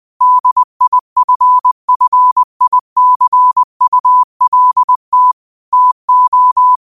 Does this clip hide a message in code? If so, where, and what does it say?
Yes: Morse code from 0.2 to 6.8 s, heard as a keyed tone, about 20 wpm, 1000 Hz. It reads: DIFFICULT TO